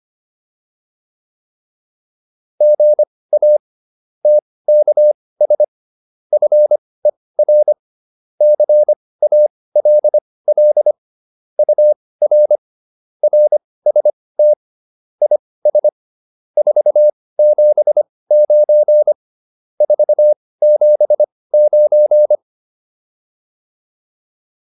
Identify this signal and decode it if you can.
Morse code, heard as a keyed tone, about 25 wpm, spaced as for 18 wpm, 600 Hz. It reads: GA TKS FER CALL UR RST IS 479 479
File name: QSO401 18